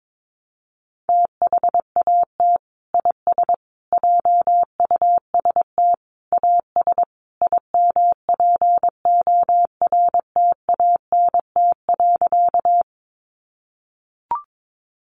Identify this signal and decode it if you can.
Morse code, heard as a keyed tone, about 22 wpm, 700 Hz. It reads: THAT IS JUST AS IMPORTANT.
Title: Sentences from Top 600 Words 22wpm